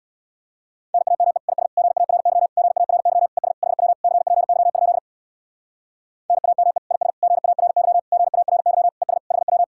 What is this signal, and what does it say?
Morse code, heard as a keyed tone, about 65 wpm, 700 Hz. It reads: NAME IS GARY GARY ES HW CPY? NAME IS GARY GARY ES HW